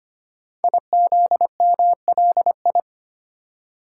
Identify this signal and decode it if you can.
Morse code, heard as a keyed tone, about 25 wpm, 700 Hz. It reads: IZMLI
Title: Code Groups - 5 Letters - 25wpm